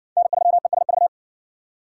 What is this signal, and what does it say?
Morse code, heard as a keyed tone, about 52 wpm, 700 Hz. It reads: N3ESV